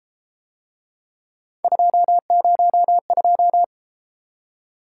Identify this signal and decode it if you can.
Morse code, heard as a keyed tone, about 32 wpm, 700 Hz. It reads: 202